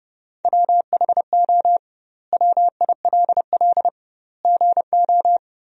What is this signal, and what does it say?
Morse code, heard as a keyed tone, about 30 wpm, 700 Hz. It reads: WHO WILL GO
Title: 2x Word-Spacing - Sentences from Top 100 Words - 30wpm